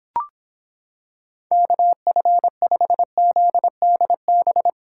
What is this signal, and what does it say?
Morse code, heard as a keyed tone, about 26 wpm, 700 Hz. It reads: KF5ZDB